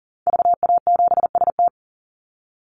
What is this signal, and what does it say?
Morse code, heard as a keyed tone, about 40 wpm, 700 Hz. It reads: VA7ST
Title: Call Signs Repeated 3x - 40wpm